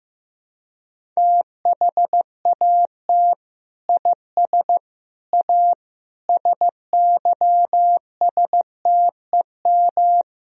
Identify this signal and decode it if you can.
Morse code, heard as a keyed tone, about 15 wpm, 700 Hz. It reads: THAT IS A SYSTEM
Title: Sentences from Top 1000 Words - Review Part 3 - 15wpm